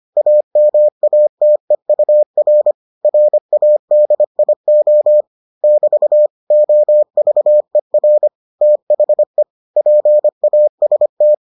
Morse code, heard as a keyed tone, about 25 wpm, 600 Hz. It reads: AMATEUR RADIO <BT> OVER THE PAST